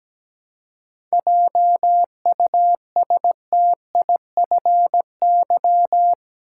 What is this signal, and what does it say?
Morse code, heard as a keyed tone, about 17 wpm, 700 Hz. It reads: JUSTIFY
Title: Seven Letter Words - 17wpm